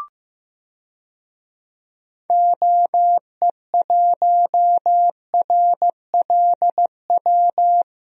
Morse code, heard as a keyed tone, about 15 wpm, 700 Hz. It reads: OE1RLW
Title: DXCC-100 Call Signs - 6 Characters - Repeated 3x - 15wpm